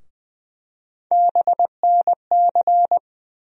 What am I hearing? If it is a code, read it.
Morse code, heard as a keyed tone, about 20 wpm, 700 Hz. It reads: BNC
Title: USA Call Sign Postfixes - 3 Letters 20wpm